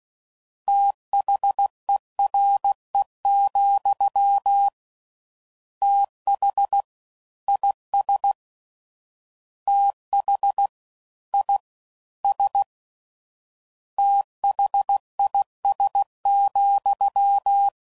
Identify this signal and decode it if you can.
Morse code, heard as a keyed tone, about 16 wpm, 800 Hz. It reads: THERE, TH IS TH I S THIS,
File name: Prefixes / FLs & THs / Fast TH